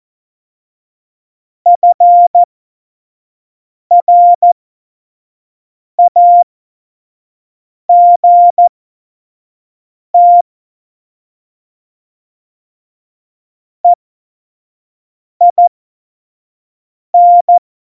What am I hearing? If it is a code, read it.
Morse code, heard as a keyed tone, about 14 wpm, spaced as for 5 wpm, 700 Hz. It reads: FRAGT EIN
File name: Morsekurs / Witze / Witz 27. Oktober 2020, 25 F70